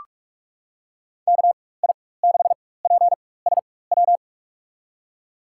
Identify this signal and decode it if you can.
Morse code, heard as a keyed tone, about 46 wpm, spaced as for 21 wpm, 700 Hz. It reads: KI6PSW
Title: USA Call Signs - 4x Character Spacing - 45wpm